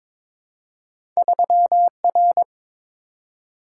Morse code, heard as a keyed tone, about 22 wpm, 700 Hz. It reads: 3R
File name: practice / 22wpm / Two Letter-Number Combinations - 22wpm